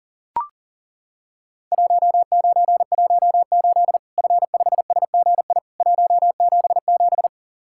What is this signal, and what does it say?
Morse code, heard as a keyed tone, about 40 wpm, 700 Hz. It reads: 1918 F5SGI 177
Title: Worked All Europe DX Contest - QTC Exchange - 40wpm